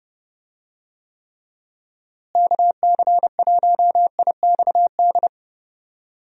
Morse code, heard as a keyed tone, about 30 wpm, 700 Hz. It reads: KC1IXD
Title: USA Call Signs - 6 Characters - Repeated 3x - 30wpm